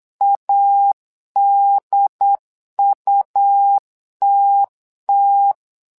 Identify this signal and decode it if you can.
Morse code, heard as a keyed tone, about 8 wpm, 800 Hz. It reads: ADUTT
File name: LEZIONI / 2 UHDN / 2 UHDN 8wpm2232